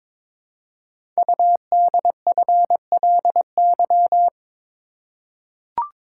Morse code, heard as a keyed tone, about 22 wpm, 700 Hz. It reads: UDFLY
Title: Code Groups - 5 Letters - 22wpm